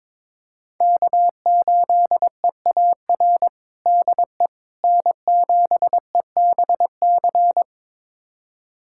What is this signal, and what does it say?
Morse code, heard as a keyed tone, about 22 wpm, 700 Hz. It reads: K8EAR DE N7EBC
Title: USA Callsign DE Callsign - 5 Characters - 22wpm